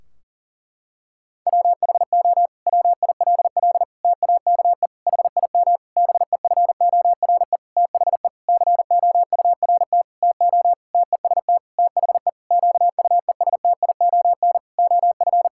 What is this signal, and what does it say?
Morse code, heard as a keyed tone, about 40 wpm, 700 Hz. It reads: WHO WILL TAKE HIM BEFORE THE COURT TO TEST THE QUESTION OF
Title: Sentences from Top 1600 Words 40wpm